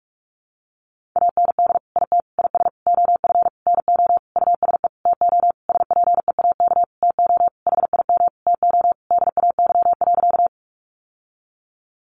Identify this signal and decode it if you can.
Morse code, heard as a keyed tone, about 44 wpm, 700 Hz. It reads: AND IT IS OF NO USE TO SPEAK TO HIM TO DAY.
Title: Sentences from Top 1100 Words 45wpm